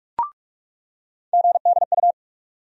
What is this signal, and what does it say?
Morse code, heard as a keyed tone, about 46 wpm, 700 Hz. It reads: GDU
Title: USA Call Sign Postfixes - 3 Letters 45wpm